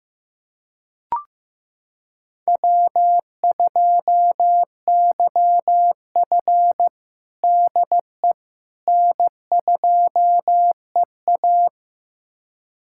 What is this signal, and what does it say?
Morse code, heard as a keyed tone, about 16 wpm, 700 Hz. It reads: W2YF DE N2EA
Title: USA Callsign DE Callsign - 4 Characters - 15wpm